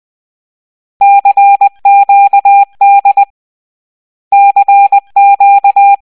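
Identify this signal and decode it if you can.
Morse code, heard as a keyed tone, about 20 wpm, 800 Hz. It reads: CQD CQ
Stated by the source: Morse code from the soundtrack of the Wii game Raving Rabbids: Travel in Time.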